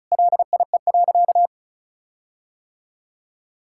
Morse code, heard as a keyed tone, about 36 wpm, 700 Hz. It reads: LIE.
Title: Sentences from Top 1600 Words 35wpm